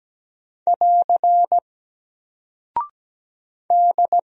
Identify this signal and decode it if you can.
Morse code, heard as a keyed tone, about 17 wpm, 700 Hz. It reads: <AR> D
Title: Single Character (Letter:Number:Punctuation:Prosign) Alt 17wpm